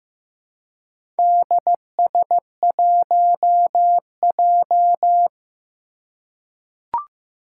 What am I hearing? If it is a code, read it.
Morse code, heard as a keyed tone, about 15 wpm, 700 Hz. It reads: DS1J
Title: Four Challenging Letter-Number Combinations 15wpm